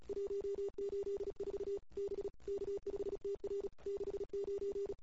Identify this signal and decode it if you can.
Morse code, heard as a keyed tone, about 34 wpm, 410 Hz. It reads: 184 B K5TR 69
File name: CW / 2008-ss-cw-k5tr-op-n5rz -22:00